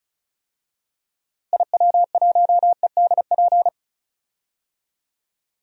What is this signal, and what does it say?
Morse code, heard as a keyed tone, about 34 wpm, 700 Hz. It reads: IW1EDP
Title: DXCC-100 Call Signs - 6 Characters - Repeated 3x - 35wpm